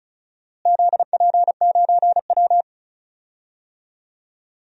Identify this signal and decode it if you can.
Morse code, heard as a keyed tone, about 34 wpm, 700 Hz. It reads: ZP9W
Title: DXCC-100 Call Signs - 4 Characters - Repeated 3x - 35wpm